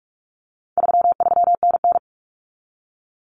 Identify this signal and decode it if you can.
Morse code, heard as a keyed tone, about 44 wpm, 700 Hz. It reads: 33NN
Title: Common Cut Numbers - 4-Digit 45wpm